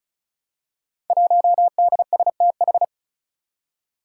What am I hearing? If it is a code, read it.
Morse code, heard as a keyed tone, about 36 wpm, 700 Hz. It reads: 1DSTH
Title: Five Challenging Letter-Number Combinations 35wpm